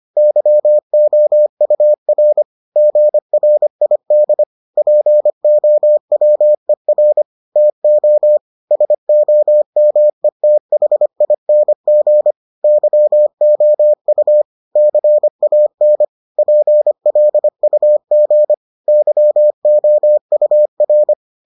Morse code, heard as a keyed tone, about 25 wpm, 600 Hz. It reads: YOUR GRID POWER TO SOMETHING YOU CAN PLUG YOUR